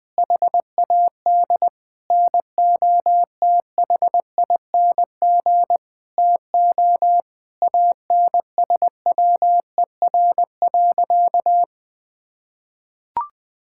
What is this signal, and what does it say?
Morse code, heard as a keyed tone, about 20 wpm, 700 Hz. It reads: HAD NOTHING TO ANSWER.
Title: Sentences from Top 1600 Words 20wpm